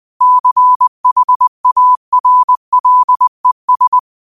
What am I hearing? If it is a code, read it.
Morse code, heard as a keyed tone, about 20 wpm, 1000 Hz. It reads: CHARLES
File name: qotdfiles / 20wpm